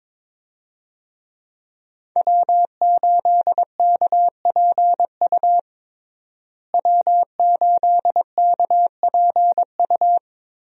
Morse code, heard as a keyed tone, about 22 wpm, 700 Hz. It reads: W8KPU W8KPU
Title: USA Call Signs - 5 Characters - Repeated 2x - 22wpm